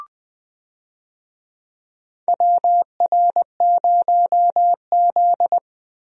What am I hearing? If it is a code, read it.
Morse code, heard as a keyed tone, about 20 wpm, 700 Hz. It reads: WR0Z
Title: USA Call Signs - 4 Characters - Repeated 2x - 20wpm